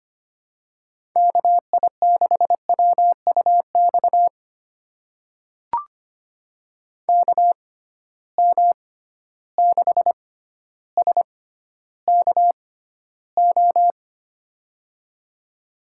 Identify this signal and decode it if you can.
Morse code, heard as a keyed tone, about 25 wpm, spaced as for 13 wpm, 700 Hz. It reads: KI6WUX KM6SKO
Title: USA Call Signs - 6x Character Spacing - 25wpm